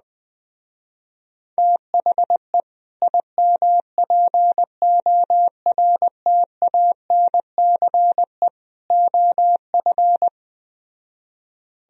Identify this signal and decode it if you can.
Morse code, heard as a keyed tone, about 21 wpm, 700 Hz. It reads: THE IMPORTANCE OF